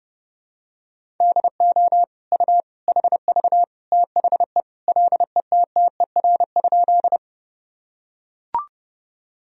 Morse code, heard as a keyed tone, about 29 wpm, 700 Hz. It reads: DO U HV THE LETTER?
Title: Sentences from Top 300 Words and Abbreviations 30wpm